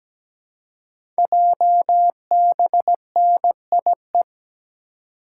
Morse code, heard as a keyed tone, about 17 wpm, 700 Hz. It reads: JBNIE
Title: Code Groups - 5 Letters - 17wpm